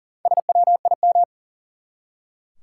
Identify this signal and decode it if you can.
Morse code, heard as a keyed tone, about 40 wpm, 700 Hz. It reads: SWIM
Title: Four Letter Words - 40wpm